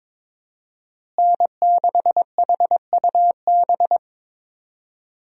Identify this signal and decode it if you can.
Morse code, heard as a keyed tone, about 22 wpm, 700 Hz. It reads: N6HUB
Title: USA Call Signs - 5 Characters - Repeated 3x - 22wpm